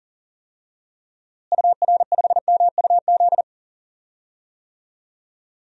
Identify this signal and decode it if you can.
Morse code, heard as a keyed tone, about 40 wpm, 700 Hz. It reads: UR5MUZ
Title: DXCC-100 Call Signs - 6 Characters - Repeated 2x - 40wpm